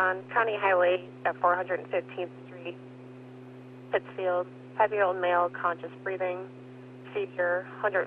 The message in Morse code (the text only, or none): none